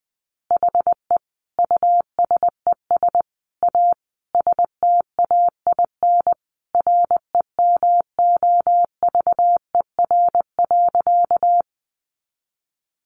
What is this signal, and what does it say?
Morse code, heard as a keyed tone, about 20 wpm, 700 Hz. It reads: HE USES A STAIN REMOVER.
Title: Additional Easy Phrases 20wpm